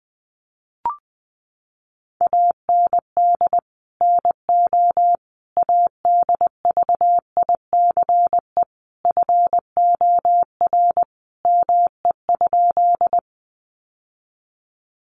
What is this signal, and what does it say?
Morse code, heard as a keyed tone, about 20 wpm, 700 Hz. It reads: AND NO ADVICE FOR ME?
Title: Sentences from Top 1100 Words 20wpm